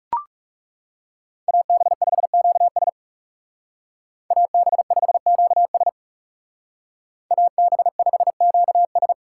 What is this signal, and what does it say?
Morse code, heard as a keyed tone, about 37 wpm, 700 Hz. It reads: AB5QS AB5QS AB5QS
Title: Speed Racing - USA 2x2 Call Signs 30wpm